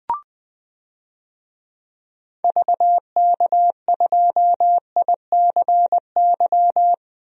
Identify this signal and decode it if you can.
Morse code, heard as a keyed tone, about 21 wpm, 700 Hz. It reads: VK2ICY
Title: DXCC-100 Call Signs - 6 Characters - 20wpm